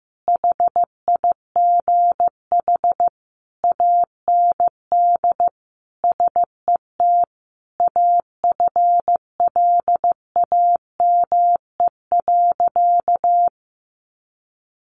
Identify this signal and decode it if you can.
Morse code, heard as a keyed tone, about 15 wpm, 700 Hz. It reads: HIGH AND SET AFLAME.
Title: A Sound Of Thunder 15wpm